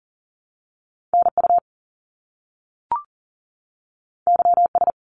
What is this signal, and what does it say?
Morse code, heard as a keyed tone, about 40 wpm, 700 Hz. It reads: NU YS
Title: Top Two Letter Combinations in Words 40wpm